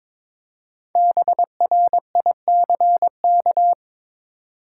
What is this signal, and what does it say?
Morse code, heard as a keyed tone, about 22 wpm, 700 Hz. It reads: BRICK
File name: 5 Letter Words - First 3 Letters Given - 22wpm